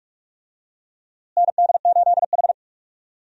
Morse code, heard as a keyed tone, about 45 wpm, 700 Hz. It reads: ND8H
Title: USA Call Signs - 4 Characters - Repeated 3x - 45wpm